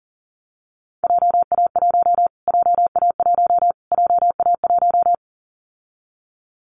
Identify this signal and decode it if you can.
Morse code, heard as a keyed tone, about 38 wpm, 700 Hz. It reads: JA1 JA1 JA1
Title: Callsign Prefixes for Top 25 DXCC - Repeated 3x - 40wpm